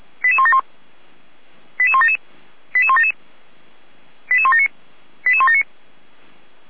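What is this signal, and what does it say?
Five-tone selective call (ZVEI), heard as quick runs of five tones, tones of about 70 ms. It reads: 88171 88177 88177 88179 88179
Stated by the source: ZVEI five-tone call received by SDR.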